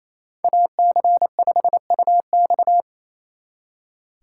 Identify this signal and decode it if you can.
Morse code, heard as a keyed tone, about 33 wpm, 700 Hz. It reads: AC5UX
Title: Speed Racing - USA 2x2 Call Signs 28wpm